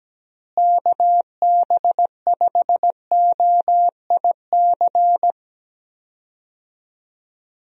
Morse code, heard as a keyed tone, about 17 wpm, 700 Hz. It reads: KB5OIC
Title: USA Call Signs - 6 Characters - Repeated 2x - 17wpm